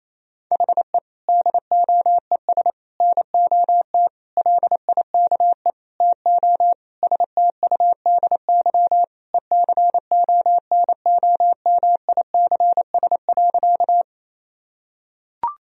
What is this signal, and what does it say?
Morse code, heard as a keyed tone, about 28 wpm, 700 Hz. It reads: HE DOES NOT LIKE TO STUDY ECONOMICS.
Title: Sentences from Top 1000 Words - Review Part 3 - 28wpm